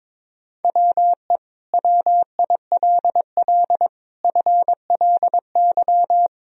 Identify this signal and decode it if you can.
Morse code, heard as a keyed tone, about 22 wpm, 700 Hz. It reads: WE WILL FLY
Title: Sentences from Top 1000 Words - Review Part 2 - 22wpm